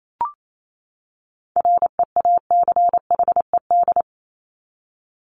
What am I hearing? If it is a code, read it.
Morse code, heard as a keyed tone, about 28 wpm, 700 Hz. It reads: REACHED
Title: ED Suffix - Top 100 Words with Base Words 28wpm